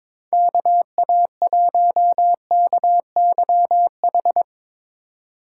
Morse code, heard as a keyed tone, about 22 wpm, 700 Hz. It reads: KA1KYH
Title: USA Call Signs - 3x Character Spacing - 22wpm